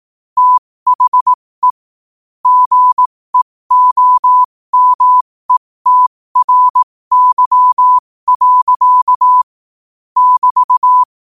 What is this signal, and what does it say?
Morse code, heard as a keyed tone, about 18 wpm, spaced as for 15 wpm, 1000 Hz. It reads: THE GEOMETRY. <BT>
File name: qotdfiles / 15wpm